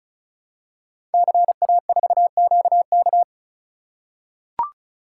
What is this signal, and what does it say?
Morse code, heard as a keyed tone, about 36 wpm, 700 Hz. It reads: CA4QK
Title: Code Groups - 5 Letter:Numbers - 35wpm